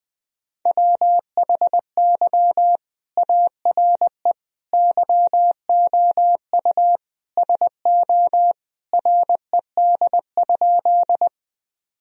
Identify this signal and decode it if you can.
Morse code, heard as a keyed tone, about 21 wpm, 700 Hz. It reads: WHY ARE YOU SO RED?
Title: Sentences from Top 1000 Words - Review Part 2 - 20wpm